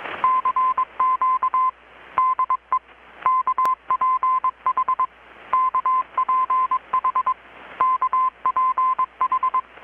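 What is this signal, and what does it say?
Morse code, heard as a keyed tone, about 22 wpm, 1030 Hz. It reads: CQ DE KPH KPH KPH
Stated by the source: Morse from KPH's Night of Nights XX maritime radiotelegraphy event, received in Chiapas, Mexico about 3,800 km away on a Kenwood R-600 with a grounded 10 m long-wire antenna.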